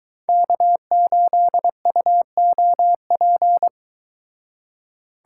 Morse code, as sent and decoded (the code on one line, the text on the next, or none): -.- ---.. ..- --- .--.
K8UOP